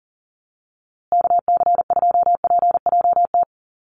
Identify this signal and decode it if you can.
Morse code, heard as a keyed tone, about 40 wpm, 700 Hz. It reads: KC2PJT
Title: USA Call Signs Repeated 2x 40wpm